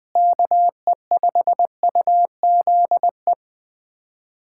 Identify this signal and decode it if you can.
Morse code, heard as a keyed tone, about 21 wpm, 700 Hz. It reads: KE5UZE
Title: USA Callsign DE Callsign - 6 Characters - 20wpm